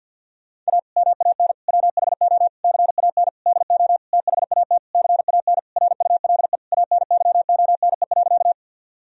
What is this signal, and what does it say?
Morse code, heard as a keyed tone, about 50 wpm, 700 Hz. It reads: A MAN WHO CAN DO THAT CAN RULE ANYONE.
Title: Sentences from Top 1000 Words - Review Part 2 - 50wpm